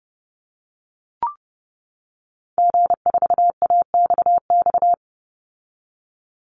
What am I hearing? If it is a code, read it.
Morse code, heard as a keyed tone, about 30 wpm, 700 Hz. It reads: G4AXX
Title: Call Signs - 30wpm